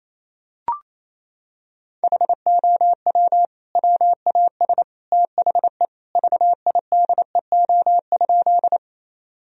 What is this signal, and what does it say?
Morse code, heard as a keyed tone, about 28 wpm, 700 Hz. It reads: HOW WAS THE VIDEO?
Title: Sentences from Top 1000 Words - Review Part 4 - 28wpm